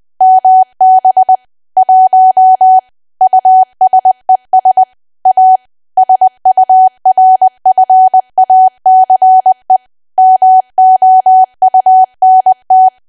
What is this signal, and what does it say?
Morse code, heard as a keyed tone, about 20 wpm, 750 Hz. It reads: MB 1 USES A SURFACE MOUNT
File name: cwa / W1AW-Level3 / 20WPM